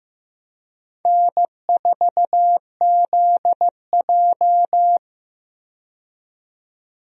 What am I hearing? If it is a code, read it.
Morse code, heard as a keyed tone, about 15 wpm, 700 Hz. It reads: N4ZJ
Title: USA Call Signs - 4 Characters - 15wpm